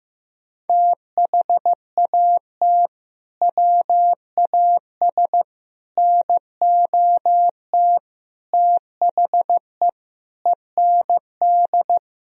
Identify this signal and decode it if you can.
Morse code, heard as a keyed tone, about 15 wpm, 700 Hz. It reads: THAT WAS NOT THE END
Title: Sentences from Top 1000 Words - Review Part 2 - 15wpm